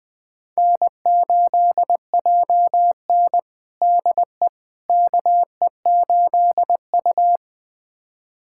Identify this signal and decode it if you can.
Morse code, heard as a keyed tone, about 19 wpm, 700 Hz. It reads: N8JN DE KE8U
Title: USA Callsign DE Callsign - 4 Characters - 20wpm